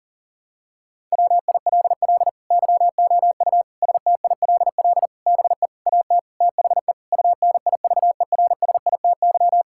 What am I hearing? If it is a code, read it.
Morse code, heard as a keyed tone, about 37 wpm, 700 Hz. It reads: WILL YOU STILL BE AT THE UNIVERSITY
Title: Sentences from Top 1000 Words - Review Part 5 - 40wpm